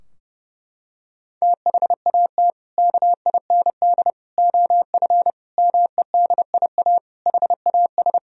Morse code, heard as a keyed tone, about 32 wpm, 700 Hz. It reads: THAT KIND OF MEDIA HAS
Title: Sentences from Top 1000 Words - Review Part 4 - 30wpm